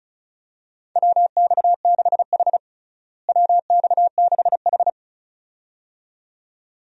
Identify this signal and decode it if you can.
Morse code, heard as a keyed tone, about 36 wpm, 700 Hz. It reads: WX6H WX6H
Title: USA Call Signs - 4 Characters - Repeated 2x - 35wpm